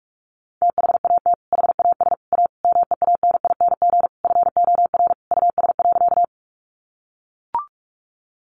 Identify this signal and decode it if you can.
Morse code, heard as a keyed tone, about 44 wpm, 700 Hz. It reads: THAT HAS A MEANING FOR US.
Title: Sentences from Top 900 Words 45wpm